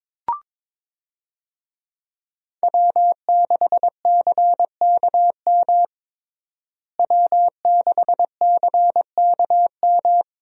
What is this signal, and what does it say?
Morse code, heard as a keyed tone, about 22 wpm, 700 Hz. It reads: W6CKM W6CKM
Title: USA Call Signs - 5 Characters - Repeated 3x - 22wpm